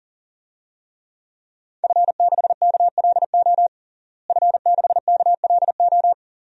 Morse code, heard as a keyed tone, about 37 wpm, 700 Hz. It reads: F6KLO F6KLO
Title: DXCC-100 Call Signs - 5 Characters - Repeated 3x - 40wpm